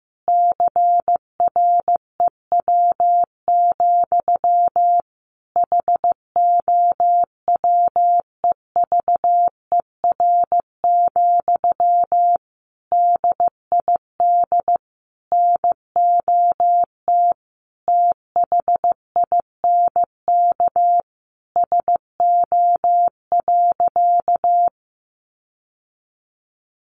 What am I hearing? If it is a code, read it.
Morse code, heard as a keyed tone, about 15 wpm, 700 Hz. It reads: CREW, HOWEVER, DID NOT THINK SO.
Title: Sentences from Top 1500 Words 15wpm